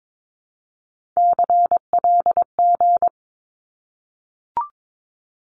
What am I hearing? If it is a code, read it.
Morse code, heard as a keyed tone, about 22 wpm, 700 Hz. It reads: CLG
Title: Abbreviations 22wpm